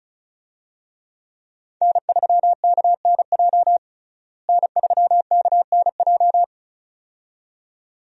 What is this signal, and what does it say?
Morse code, heard as a keyed tone, about 34 wpm, 700 Hz. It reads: N3KNJ N3KNJ
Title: USA Call Signs - 5 Characters - Repeated 2x - 35wpm